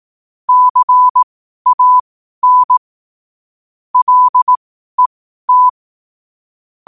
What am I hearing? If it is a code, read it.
Morse code, heard as a keyed tone, about 18 wpm, spaced as for 12 wpm, 1000 Hz. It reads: CAN LET
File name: qotdfiles / 12wpm